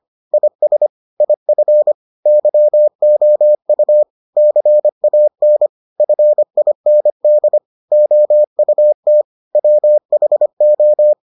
Morse code, heard as a keyed tone, about 25 wpm, 600 Hz. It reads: IS IF YOU CAN FIND OUT WHO